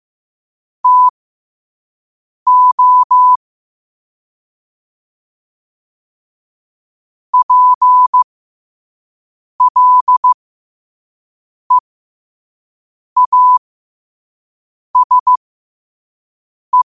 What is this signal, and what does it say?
Morse code, heard as a keyed tone, about 15 wpm, spaced as for 5 wpm, 1000 Hz. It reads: TO PLEASE
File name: qotdfiles / 5wpm